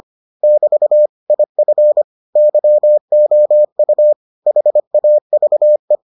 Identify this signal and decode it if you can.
Morse code, heard as a keyed tone, about 25 wpm, 600 Hz. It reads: <BT> IF YOU HAVE